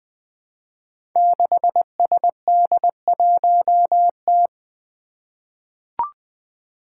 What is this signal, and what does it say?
Morse code, heard as a keyed tone, about 20 wpm, 700 Hz. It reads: 6SD1T
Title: Five Challenging Letter-Number Combinations 20wpm